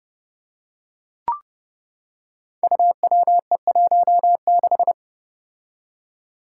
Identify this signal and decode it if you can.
Morse code, heard as a keyed tone, about 30 wpm, 700 Hz. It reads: UWE16
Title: Five Challenging Letter-Number Combinations 30wpm